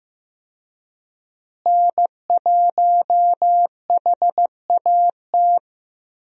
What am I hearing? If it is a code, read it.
Morse code, heard as a keyed tone, about 15 wpm, 700 Hz. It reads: N1HAT
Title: USA Call Signs - 5 Characters - Repeated 2x - 15wpm